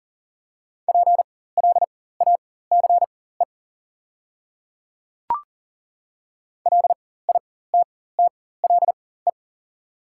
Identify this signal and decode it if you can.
Morse code, heard as a keyed tone, about 38 wpm, spaced as for 18 wpm, 700 Hz. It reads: PLACE LITTLE
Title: 4x Character Spacing - Top 200 Words - 40wpm